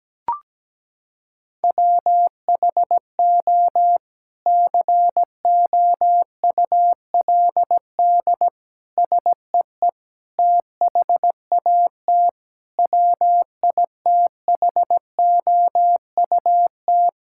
Morse code, heard as a keyed tone, about 17 wpm, 700 Hz. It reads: WHO COULD SEE THAT WITHOUT
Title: Sentences from Top 1600 Words 17wpm